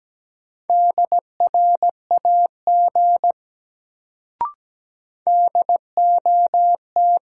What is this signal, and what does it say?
Morse code, heard as a keyed tone, about 17 wpm, 700 Hz. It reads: DRAG DOT
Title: Sentences from Top 1500 Words - New Word Drill - 17wpm